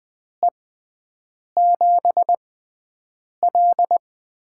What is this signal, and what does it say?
Morse code, heard as a keyed tone, about 20 wpm, 700 Hz. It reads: E 7 L
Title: USA Call Signs - 6x Character Spacing - 20wpm